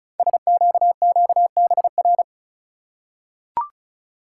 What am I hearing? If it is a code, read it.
Morse code, heard as a keyed tone, about 36 wpm, 690 Hz. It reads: SQQBR
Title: Code Groups - 5 Letters - 35wpm